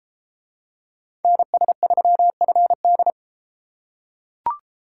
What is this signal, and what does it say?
Morse code, heard as a keyed tone, about 33 wpm, 700 Hz. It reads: NS3FD